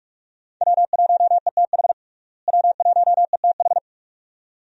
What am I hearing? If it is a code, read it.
Morse code, heard as a keyed tone, about 46 wpm, 700 Hz. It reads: W1ETH W1ETH
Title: USA Call Signs - 5 Characters - Repeated 2x - 45wpm